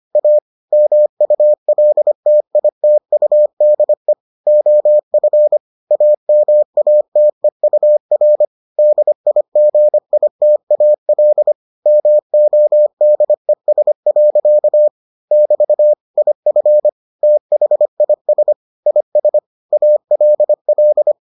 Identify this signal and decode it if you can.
Morse code, heard as a keyed tone, about 25 wpm, 600 Hz. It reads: A MULTITUDE OF AMATEUR DIGITAL MODES. <BT> IF THIS IS ALL